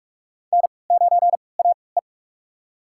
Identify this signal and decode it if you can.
Morse code, heard as a keyed tone, about 44 wpm, spaced as for 25 wpm, 700 Hz. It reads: N9AE
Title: USA Call Signs - 4 Characters - 3x Character Spacing - 45wpm